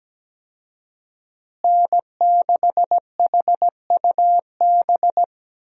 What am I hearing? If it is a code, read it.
Morse code, heard as a keyed tone, about 17 wpm, 700 Hz. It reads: N6HUB
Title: USA Call Signs - 5 Characters - Repeated 3x - 17wpm